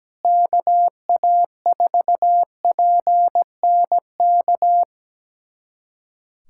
Morse code, heard as a keyed tone, about 17 wpm, 700 Hz. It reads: KA4PNK
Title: USA Callsign DE Callsign 17wpm